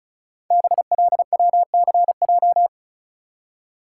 Morse code, heard as a keyed tone, about 36 wpm, 700 Hz. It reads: BLWCJ
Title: Code Groups - 5 Letters - 35wpm